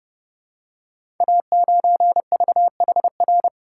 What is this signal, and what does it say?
Morse code, heard as a keyed tone, about 29 wpm, 700 Hz. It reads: A9VHR
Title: Code Groups - 5 Letter:Numbers - 30wpm